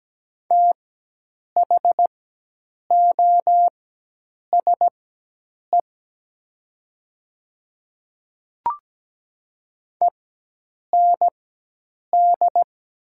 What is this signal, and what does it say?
Morse code, heard as a keyed tone, about 17 wpm, spaced as for 8 wpm, 700 Hz. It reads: THOSE END
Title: 4x Character Spacing - Top 300 Words - 17wpm